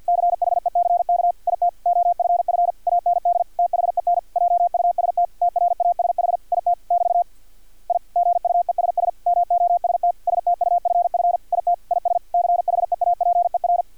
Morse code, heard as a keyed tone, about 50 wpm, 700 Hz. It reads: CHECK IT OFF AND THEN JUST TRASH IT <BT> I GUESS MOST STUFF IT IS CHEAPER
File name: archived / CRSnet